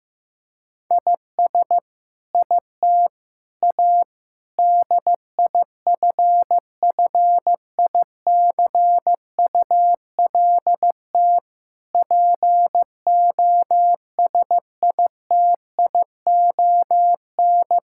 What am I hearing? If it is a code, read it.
Morse code, heard as a keyed tone, about 16 wpm, 700 Hz. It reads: IS IT A DIFFICULT POSITION
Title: Sentences from Top 1000 Words - Review Part 5 - 15wpm